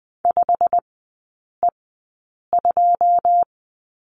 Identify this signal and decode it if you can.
Morse code, heard as a keyed tone, about 20 wpm, spaced as for 8 wpm, 700 Hz. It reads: 5E2
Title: 127 Lesson 21e - Number 6 - Three Characters 20wpm